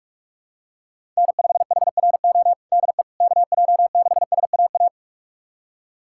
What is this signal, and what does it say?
Morse code, heard as a keyed tone, about 46 wpm, 700 Hz. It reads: N5HRO DE KJ6SAA
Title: USA Callsign DE Callsign 45wpm